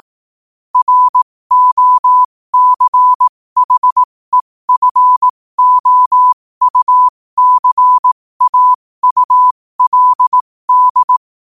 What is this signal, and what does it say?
Morse code, heard as a keyed tone, about 18 wpm, spaced as for 15 wpm, 1000 Hz. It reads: ROCHEFOUCAULD
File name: qotdfiles / 15wpm